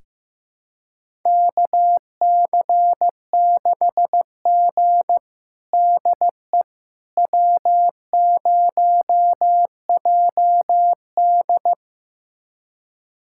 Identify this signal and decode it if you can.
Morse code, heard as a keyed tone, about 15 wpm, 700 Hz. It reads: KC6G DE W0JD
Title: USA Callsign DE Callsign - 4 Characters - 15wpm